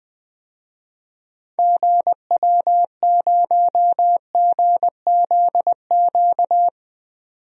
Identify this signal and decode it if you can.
Morse code, heard as a keyed tone, about 21 wpm, 700 Hz. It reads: GW0GZQ
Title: DXCC-100 Call Signs - 6 Characters - 20wpm